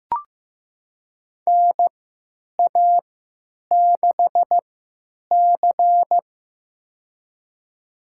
Morse code, heard as a keyed tone, about 15 wpm, spaced as for 9 wpm, 700 Hz. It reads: NA6C
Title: USA Call Signs - 4 Characters - 3x Character Spacing - 15wpm